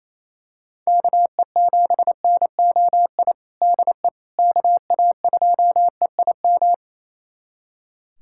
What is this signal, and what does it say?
Morse code, heard as a keyed tone, about 27 wpm, 700 Hz. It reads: KE7NOI DE KA2EIM
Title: USA Callsign DE Callsign - 6 Characters - 28wpm